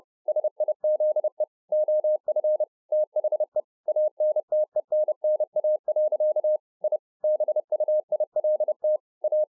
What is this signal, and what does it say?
Morse code, heard as a keyed tone, about 30 wpm, 610 Hz. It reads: SIZE OF THE ANTENNA. I BUILT A